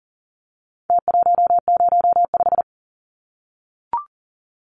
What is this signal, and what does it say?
Morse code, heard as a keyed tone, about 40 wpm, 700 Hz. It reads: T105
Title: Numbers - Contest Style - 4-Digit 40wpm